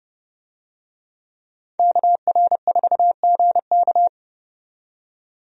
Morse code, heard as a keyed tone, about 31 wpm, 700 Hz. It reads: KR4GK
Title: USA Call Signs - 5 Characters - 30wpm